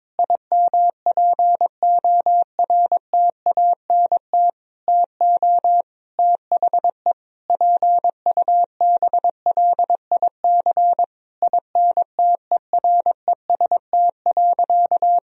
Morse code, heard as a keyed tone, about 22 wpm, 700 Hz. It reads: IMPORTANT TO THE PUBLIC INTEREST.
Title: Sentences from Top 1000 Words - Review Part 4 - 22wpm